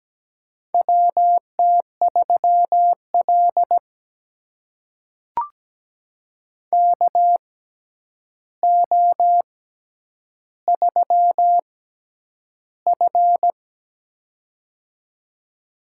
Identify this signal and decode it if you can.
Morse code, heard as a keyed tone, about 17 wpm, spaced as for 9 wpm, 700 Hz. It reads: WT3L KO3F
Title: USA Call Signs - 4 Characters - 6x Character Spacing - 17wpm